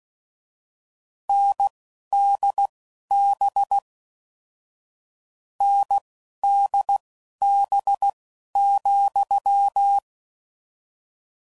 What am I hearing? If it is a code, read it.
Morse code, heard as a keyed tone, about 16 wpm, spaced as for 11 wpm, 800 Hz. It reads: NDB NDB,
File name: Combinations / Difficult Combos / NDB NDB6